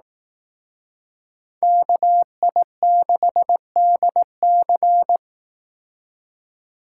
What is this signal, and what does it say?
Morse code, heard as a keyed tone, about 18 wpm, 700 Hz. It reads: KI6DC
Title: Speed Racing - USA 2x2 Call Signs 15wpm